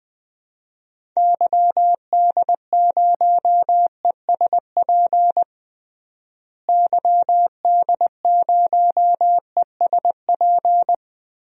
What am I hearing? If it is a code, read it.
Morse code, heard as a keyed tone, about 20 wpm, 700 Hz. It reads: YD0ESP YD0ESP
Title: DXCC-100 Call Signs - 6 Characters - Repeated 2x - 20wpm